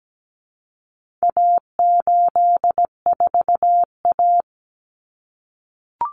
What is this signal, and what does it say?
Morse code, heard as a keyed tone, about 17 wpm, 700 Hz. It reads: A84A
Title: Common Cut Numbers - 4-Digit 17wpm